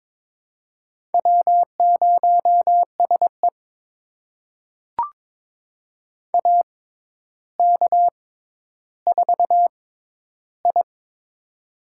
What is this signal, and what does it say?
Morse code, heard as a keyed tone, about 21 wpm, spaced as for 11 wpm, 700 Hz. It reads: W0SE AK4I
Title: USA Call Signs - 4 Characters - 6x Character Spacing - 22wpm